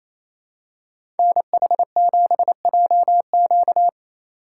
Morse code, heard as a keyed tone, about 28 wpm, 700 Hz. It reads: NH7JQ
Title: USA Call Signs - 5 Characters - 2x Character Spacing - 28wpm